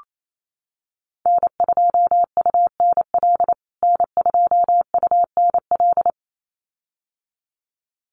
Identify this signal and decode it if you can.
Morse code, heard as a keyed tone, about 28 wpm, 700 Hz. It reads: N2UNL N2UNL
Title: USA Call Signs Repeated 2x 28wpm